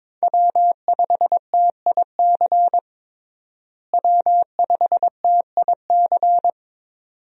Morse code, heard as a keyed tone, about 22 wpm, 700 Hz. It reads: W5TIC W5TIC
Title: USA Call Signs - 5 Characters - Repeated 2x - 22wpm